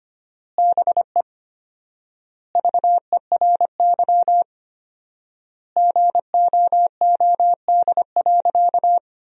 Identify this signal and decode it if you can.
Morse code, heard as a keyed tone, about 25 wpm, 700 Hz. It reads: BE VERY GOOD.
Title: 4x Word-Spacing - Sentences from Top 200 Words - 25wpm